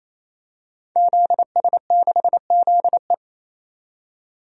Morse code, heard as a keyed tone, about 27 wpm, 700 Hz. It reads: ZS6ZE
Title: DXCC-100 Call Signs - 5 Characters - 28wpm